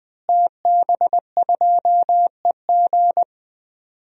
Morse code, heard as a keyed tone, about 19 wpm, 700 Hz. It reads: TB2EG
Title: DXCC-100 Call Signs - 5 Characters - Repeated 3x - 20wpm